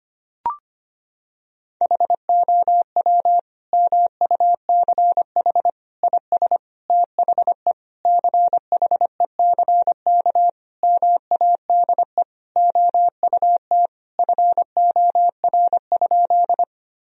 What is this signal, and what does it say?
Morse code, heard as a keyed tone, about 25 wpm, 700 Hz. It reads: HOW MUCH IS THE CHECK MADE OUT FOR?
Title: Sentences from Top 1000 Words - Review Part 3 - 25wpm